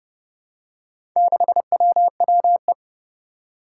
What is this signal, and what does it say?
Morse code, heard as a keyed tone, about 31 wpm, 700 Hz. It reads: 6WWE
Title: Four Challenging Letter-Number Combinations 30wpm